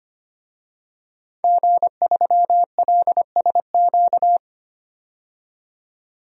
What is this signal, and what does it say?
Morse code, heard as a keyed tone, about 25 wpm, 700 Hz. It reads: G3LSQ
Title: DXCC-100 Call Signs - 5 Characters - 25wpm